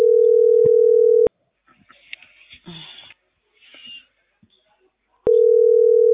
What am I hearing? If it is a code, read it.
Ringback tone (North America), 440 + 480 Hz.